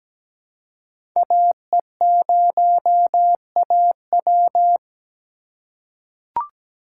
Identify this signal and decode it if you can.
Morse code, heard as a keyed tone, about 17 wpm, 700 Hz. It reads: AE0AW
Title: USA Call Signs - 5 Characters - 3x Character Spacing - 17wpm